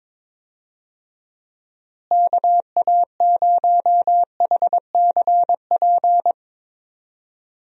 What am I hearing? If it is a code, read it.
Morse code, heard as a keyed tone, about 22 wpm, 700 Hz. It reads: KA0HCP
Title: USA Call Signs - 6 Characters - Repeated 3x - 22wpm